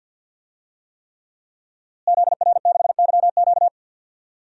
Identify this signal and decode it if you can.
Morse code, heard as a keyed tone, about 52 wpm, 700 Hz. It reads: ZR6YQ
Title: DXCC-100 Call Signs - 5 Characters - Repeated 3x - 50wpm